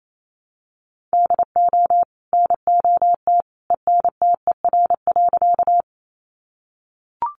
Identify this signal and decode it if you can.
Morse code, heard as a keyed tone, about 28 wpm, 700 Hz. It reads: DO NOT ENTER.
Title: Sentences from Top 1300 Words 28wpm